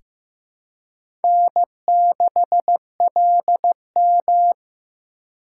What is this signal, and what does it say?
Morse code, heard as a keyed tone, about 15 wpm, 700 Hz. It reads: N6LM
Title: USA Call Signs - 4 Characters - 3x Character Spacing - 15wpm